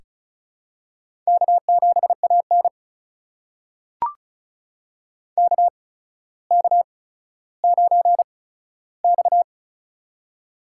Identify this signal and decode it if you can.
Morse code, heard as a keyed tone, about 36 wpm, spaced as for 15 wpm, 700 Hz. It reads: K7AN KK9X
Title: USA Call Signs - 4 Characters - 8x Character Spacing - 35wpm